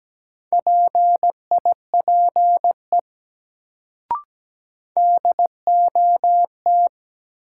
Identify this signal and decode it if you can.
Morse code, heard as a keyed tone, about 17 wpm, 700 Hz. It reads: PIPE DOT
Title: Sentences from Top 1500 Words - New Word Drill Alt - 17wpm